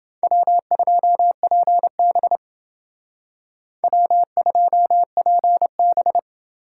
Morse code, heard as a keyed tone, about 28 wpm, 700 Hz. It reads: W2PB W2PB